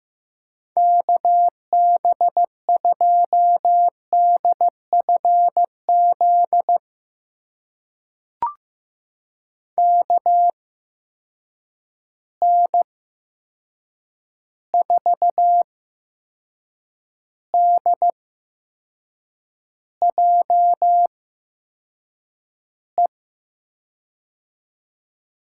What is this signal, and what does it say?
Morse code, heard as a keyed tone, about 15 wpm, spaced as for 6 wpm, 700 Hz. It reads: KB2DFZ KN4DJE
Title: USA Call Signs - 6 Characters - 8x Character Spacing - 15wpm